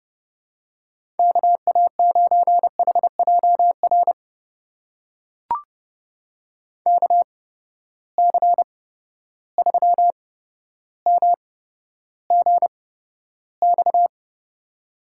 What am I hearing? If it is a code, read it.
Morse code, heard as a keyed tone, about 30 wpm, spaced as for 13 wpm, 700 Hz. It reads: KA9HJR KC3MGX
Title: USA Call Signs - 6 Characters - 8x Character Spacing - 30wpm